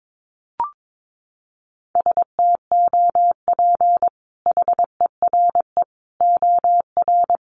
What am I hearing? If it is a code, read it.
Morse code, heard as a keyed tone, about 22 wpm, 700 Hz. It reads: STOP HERE OR
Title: Sentences from Top 1500 Words 22wpm